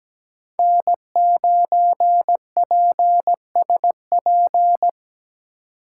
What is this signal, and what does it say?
Morse code, heard as a keyed tone, about 20 wpm, 700 Hz. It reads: N9PSP
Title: Speed Racing - USA 1x3 Call Signs 17wpm